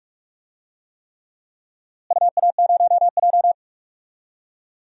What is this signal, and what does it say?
Morse code, heard as a keyed tone, about 42 wpm, 700 Hz. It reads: UA0J
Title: DXCC-100 Call Signs - 4 Characters - Repeated 3x - 45wpm